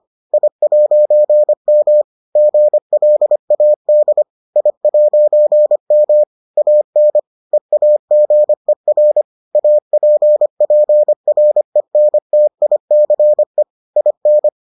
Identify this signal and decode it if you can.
Morse code, heard as a keyed tone, about 25 wpm, 600 Hz. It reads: I'M GLAD I'M AN EAGER APPRENTICE IN